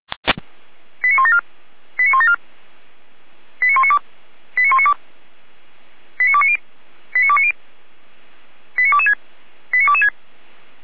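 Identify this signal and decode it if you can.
Five-tone selective call (ZVEI), heard as quick runs of five tones, tones of about 70 ms. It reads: 89174 89174 89192 89192 89290 89290 89226 89226
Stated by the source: ZVEI five-tone call 89391 received with an SDR.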